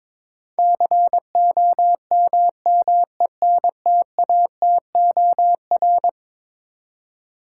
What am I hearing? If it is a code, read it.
Morse code, heard as a keyed tone, about 22 wpm, 700 Hz. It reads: COMMENTATOR
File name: practice / 22wpm / Eleven Letter Words - 22wpm